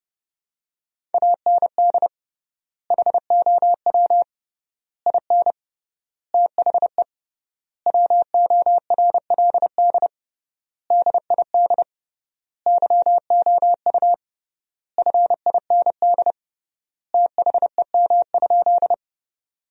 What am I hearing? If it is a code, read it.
Morse code, heard as a keyed tone, about 32 wpm, 700 Hz. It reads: AND HOW IN THE WORLD DID YOU FIND THEM?
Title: 3x Word-Spacing - Sentences from Top 200 Words - 30wpm